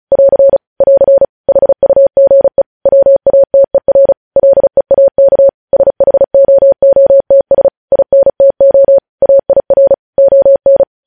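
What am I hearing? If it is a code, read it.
Morse code, heard as a keyed tone, about 36 wpm, 550 Hz. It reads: <AR> <AR> HUGE WATER LEAK SHOOTS INTO AIR ON
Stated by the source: Morse code news headlines sent at 35 words per minute.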